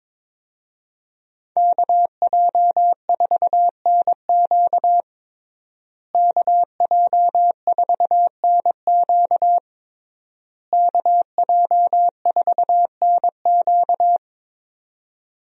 Morse code, heard as a keyed tone, about 22 wpm, 700 Hz. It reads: KJ4NQ KJ4NQ KJ4NQ
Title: USA Call Signs - 5 Characters - Repeated 3x - 22wpm